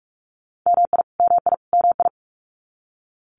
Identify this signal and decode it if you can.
Morse code, heard as a keyed tone, about 44 wpm, 700 Hz. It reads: MI MI MI
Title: US States Repeated 3x 45wpm